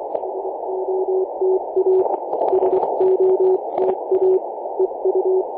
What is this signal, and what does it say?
Morse code, heard as a keyed tone, about 24 wpm, 380 Hz, with noise. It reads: SOTA SOTA EU
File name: posts / may-01-2117z-vk3qb-vk9n-18-073mhz-cw